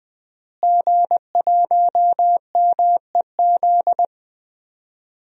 Morse code, heard as a keyed tone, about 20 wpm, 700 Hz. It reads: G1MEZ
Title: DXCC-100 Call Signs - 5 Characters - Repeated 2x - 20wpm